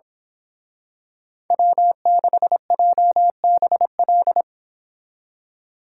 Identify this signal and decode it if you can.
Morse code, heard as a keyed tone, about 26 wpm, 700 Hz. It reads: W6JBL
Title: Speed Racing - USA 1x3 Call Signs 22wpm